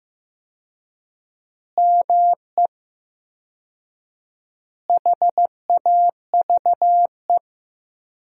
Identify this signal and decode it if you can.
Morse code, heard as a keyed tone, about 15 wpm, 700 Hz. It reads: ME HAVE
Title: 4x Word-Spacing - Sentences from Top 300 Words - 15wpm